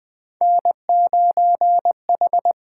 Morse code, heard as a keyed tone, about 20 wpm, 700 Hz. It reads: N9H
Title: USA Call Signs - 4 Characters - 8x Character Spacing - 20wpm